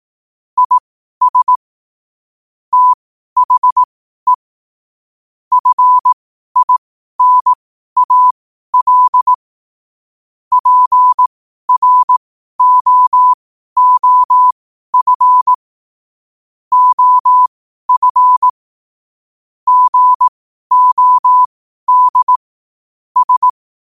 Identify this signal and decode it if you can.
Morse code, heard as a keyed tone, about 18 wpm, spaced as for 12 wpm, 1000 Hz. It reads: IS THE FINAL PROOF OF GOD S
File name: qotdfiles / 12wpm